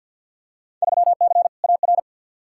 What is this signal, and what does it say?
Morse code, heard as a keyed tone, about 53 wpm, 700 Hz. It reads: 3C AR
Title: Field Day Exchanges Encore - 50wpm